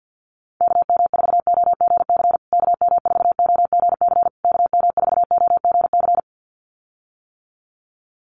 Morse code, heard as a keyed tone, about 48 wpm, 700 Hz. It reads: KM4OGC KM4OGC KM4OGC
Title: USA Call Signs Repeated 3x 50wpm